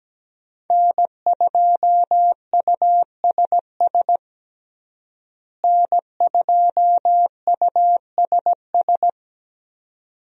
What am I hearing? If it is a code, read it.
Morse code, heard as a keyed tone, about 17 wpm, 700 Hz. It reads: N2USS N2USS
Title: USA Call Signs - 5 Characters - Repeated 2x - 17wpm